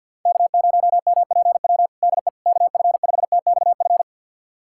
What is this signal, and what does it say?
Morse code, heard as a keyed tone, about 46 wpm, 700 Hz. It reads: K0MPW DE KF5TXF
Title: USA Callsign DE Callsign 50wpm